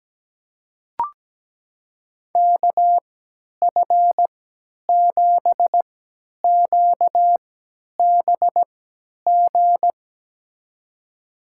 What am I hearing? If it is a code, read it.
Morse code, heard as a keyed tone, about 17 wpm, spaced as for 10 wpm, 700 Hz. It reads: KF7QBG
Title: USA Call Signs - 3x Character Spacing - 17wpm